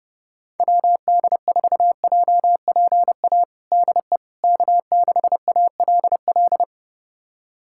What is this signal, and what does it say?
Morse code, heard as a keyed tone, about 30 wpm, 700 Hz. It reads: WD4JPA DE K6ALL
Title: USA Callsign DE Callsign 30wpm